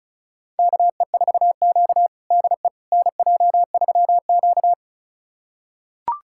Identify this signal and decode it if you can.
Morse code, heard as a keyed tone, about 36 wpm, 700 Hz. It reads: KE4Q DE NJ3Q
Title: USA Callsign DE Callsign - 4 Characters - 35wpm